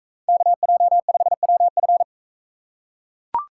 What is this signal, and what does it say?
Morse code, heard as a keyed tone, about 40 wpm, 700 Hz. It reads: KJ5WF